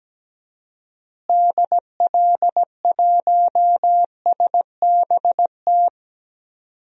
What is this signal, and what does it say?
Morse code, heard as a keyed tone, about 17 wpm, 700 Hz. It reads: DL1SBT
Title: DXCC-100 Call Signs - 6 Characters - Repeated 3x - 17wpm